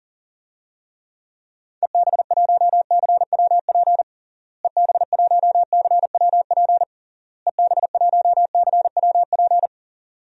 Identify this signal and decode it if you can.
Morse code, heard as a keyed tone, about 40 wpm, 700 Hz. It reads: EB1CWP EB1CWP EB1CWP
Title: DXCC-100 Call Signs - 6 Characters - Repeated 3x - 40wpm